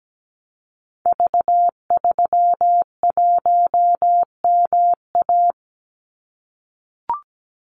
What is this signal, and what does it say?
Morse code, heard as a keyed tone, about 17 wpm, 700 Hz. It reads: V31MA
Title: Call Signs - 17wpm